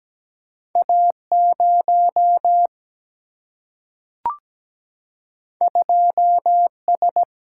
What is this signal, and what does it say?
Morse code, heard as a keyed tone, about 17 wpm, 700 Hz. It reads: A0 2S
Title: Single-Letter Single-Number 17wpm